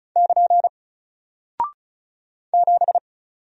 Morse code, heard as a keyed tone, about 36 wpm, 700 Hz. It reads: <KN> 7
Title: Single Character (Letter:Number:Punctuation:Prosign) 35wpm